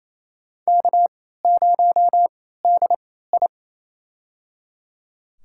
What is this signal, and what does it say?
Morse code, heard as a keyed tone, about 28 wpm, spaced as for 16 wpm, 700 Hz. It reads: K0DI
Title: USA Call Signs - 4 Characters - 3x Character Spacing - 28wpm